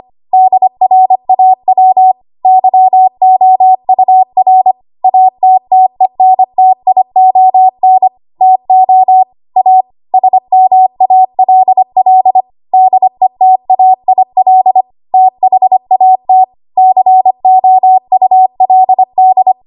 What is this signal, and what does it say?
Morse code, heard as a keyed tone, about 25 wpm, 750 Hz. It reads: DRAW YOUR ATTENTION TO A SMALL DETAIL THAT COULD
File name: Archive / 25 WPM / 25WPM